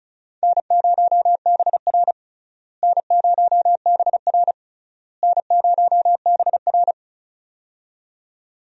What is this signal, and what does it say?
Morse code, heard as a keyed tone, about 36 wpm, 700 Hz. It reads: N0BR N0BR N0BR
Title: USA Call Signs - 4 Characters - Repeated 3x - 35wpm